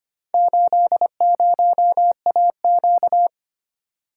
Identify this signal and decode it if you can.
Morse code, heard as a keyed tone, about 25 wpm, 700 Hz. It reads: 80AQ
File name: practice / 25wpm / Four Letter-Number Combinations - 25wpm